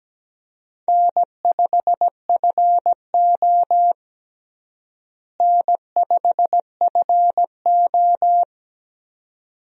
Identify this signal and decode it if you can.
Morse code, heard as a keyed tone, about 17 wpm, 700 Hz. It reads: N5FO N5FO
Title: USA Call Signs - 4 Characters - Repeated 2x - 17wpm